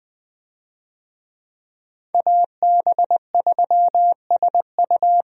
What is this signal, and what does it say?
Morse code, heard as a keyed tone, about 21 wpm, 700 Hz. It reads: AB3SU
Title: USA Call Signs - 5 Characters - Repeated 3x - 20wpm